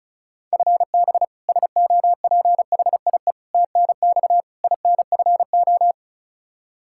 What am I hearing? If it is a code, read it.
Morse code, heard as a keyed tone, about 36 wpm, 700 Hz. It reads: FB SOPHIE TNX INFO
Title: K1USN SST QSO Practice 35wpm